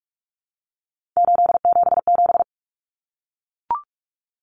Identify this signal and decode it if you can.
Morse code, heard as a keyed tone, about 44 wpm, 700 Hz. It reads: 877
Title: Common Cut Numbers - 3-Digit 45wpm